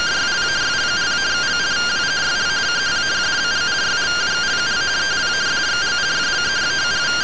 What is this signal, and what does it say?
Two-tone teleprinter signal, about 170 Hz shift.
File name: findings / 202 coronal iodine 3mm.gen-02 rtty rx